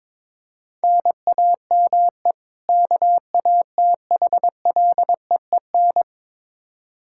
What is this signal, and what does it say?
Morse code, heard as a keyed tone, about 21 wpm, 700 Hz. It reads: NAME KATHLEEN
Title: Practice QSO - Exchange 1 - Beginning - 22wpm